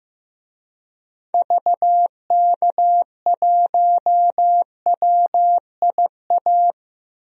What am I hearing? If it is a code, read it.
Morse code, heard as a keyed tone, about 16 wpm, 700 Hz. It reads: VK1WIA
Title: DXCC-100 Call Signs - 6 Characters - Repeated 3x - 15wpm